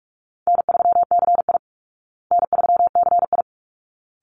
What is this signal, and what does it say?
Morse code, heard as a keyed tone, about 44 wpm, 700 Hz. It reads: N3CI N3CI
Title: Easy Call Signs Repeated 2x - 45wpm